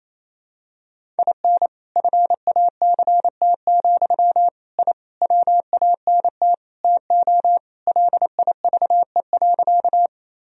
Morse code, heard as a keyed tone, about 28 wpm, 700 Hz. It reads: IN FACT, I WANT TO LIVE.
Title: Sentences from Top 1000 Words - Review Part 3 - 28wpm